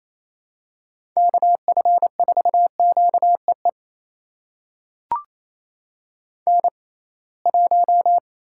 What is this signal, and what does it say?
Morse code, heard as a keyed tone, about 28 wpm, spaced as for 21 wpm, 700 Hz. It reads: KF4QEE N1
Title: USA Call Signs - 6x Character Spacing - 28wpm